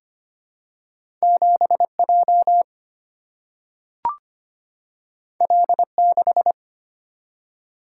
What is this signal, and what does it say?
Morse code, heard as a keyed tone, about 25 wpm, 700 Hz. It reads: 7J L6
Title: Single-Letter Single-Number 25wpm